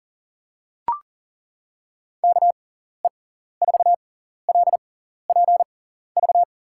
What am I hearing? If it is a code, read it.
Morse code, heard as a keyed tone, about 40 wpm, spaced as for 14 wpm, 700 Hz. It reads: KE4LPV
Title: USA Call Signs - 6 Characters - 6x Character Spacing - 40wpm